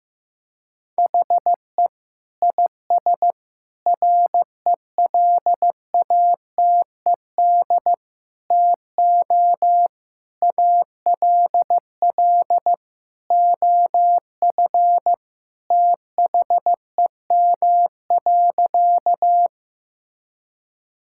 Morse code, heard as a keyed tone, about 15 wpm, 700 Hz. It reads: HE IS RELATED TO ALL OF THEM.
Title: Sentences from Top 1000 Words - Review Part 5 - 15wpm